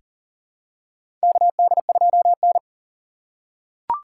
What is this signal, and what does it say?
Morse code, heard as a keyed tone, about 40 wpm, 700 Hz. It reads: KD2N
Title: USA Call Signs - 4 Characters - 6x Character Spacing - 40wpm